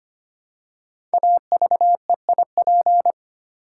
Morse code, heard as a keyed tone, about 25 wpm, 700 Hz. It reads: AVEIP
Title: Code Groups - 5 Letters - 25wpm